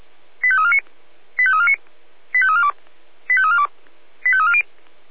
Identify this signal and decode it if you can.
Five-tone selective call (ZVEI), heard as quick runs of five tones, tones of about 80 ms. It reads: 85349 85349 85342 85342 85340
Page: SDR/ZVEI